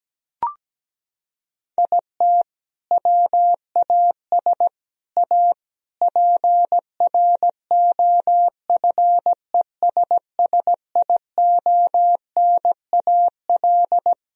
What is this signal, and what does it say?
Morse code, heard as a keyed tone, about 17 wpm, 700 Hz. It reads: IT WAS A PROFESSIONAL
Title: Sentences from Top 1000 Words - Review Part 4 - 17wpm